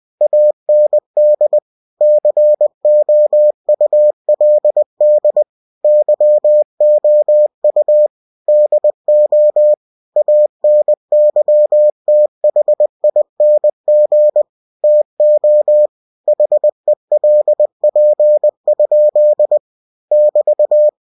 Morse code, heard as a keyed tone, about 20 wpm, 600 Hz. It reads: AND COULD YOU DO ANYTHING TO HELP? <BT>
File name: cw-books / winnie-the-pooh / winnie-20wpm-page-0089